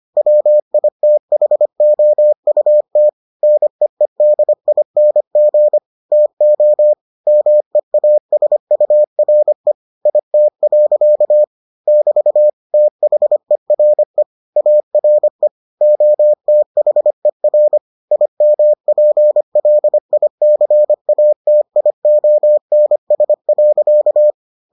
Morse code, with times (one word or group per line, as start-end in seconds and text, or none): 0.2-3.1 WITHOUT
3.4-5.8 NEEDING
6.1-6.9 TO
7.3-9.7 MEASURE
10.1-11.4 IT.
11.9-12.5 <BT>
12.7-14.2 THERE
14.6-15.5 ARE
15.8-17.8 OTHER
18.1-24.3 IMPLICATIONS.